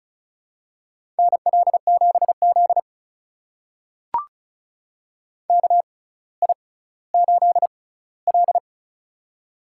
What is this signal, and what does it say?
Morse code, heard as a keyed tone, about 34 wpm, spaced as for 18 wpm, 700 Hz. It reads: NL7Z KI8L
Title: USA Call Signs - 4 Characters - 6x Character Spacing - 35wpm